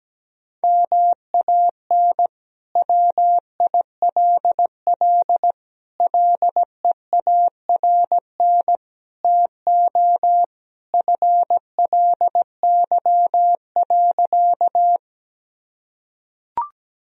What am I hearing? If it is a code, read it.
Morse code, heard as a keyed tone, about 17 wpm, 700 Hz. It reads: MAN WILL LEARN TO FLY.
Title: Sentences from Top 1000 Words - Review Part 2 - 17wpm